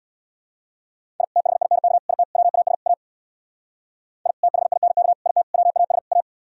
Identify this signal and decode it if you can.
Morse code, heard as a keyed tone, about 66 wpm, 700 Hz. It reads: I THINK IT WAS A I THINK IT WAS A